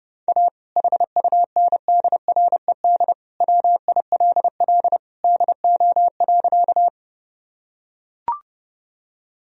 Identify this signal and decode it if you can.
Morse code, heard as a keyed tone, about 30 wpm, 700 Hz. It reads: A HUNDRED WILL DO.
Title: Sentences from Top 1000 Words - Review Part 2 - 30wpm